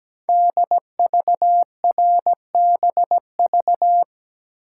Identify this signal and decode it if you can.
Morse code, heard as a keyed tone, about 17 wpm, 700 Hz. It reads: DVRBV
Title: Code Groups - 5 Letters - 17wpm